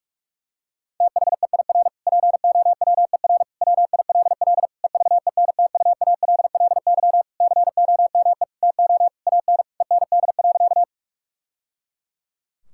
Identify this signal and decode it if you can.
Morse code, heard as a keyed tone, about 46 wpm, 700 Hz. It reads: THEIR POWER WILL EVENTUALLY COME TO AN END.
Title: Sentences from Top 1000 Words - Review Part 6 - 45wpm